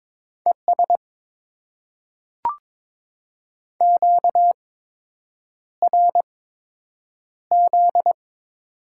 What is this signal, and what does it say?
Morse code, heard as a keyed tone, about 22 wpm, spaced as for 8 wpm, 700 Hz. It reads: ES QRZ
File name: Top 90 QSO Elements - 8x Character Spacing - 22wpm